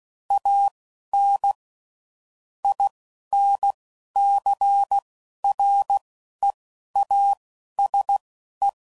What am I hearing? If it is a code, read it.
Morse code, heard as a keyed tone, about 16 wpm, spaced as for 11 wpm, 800 Hz. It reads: AN INCREASE
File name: CW Code Groups @10WPM / Technical Subjects / XC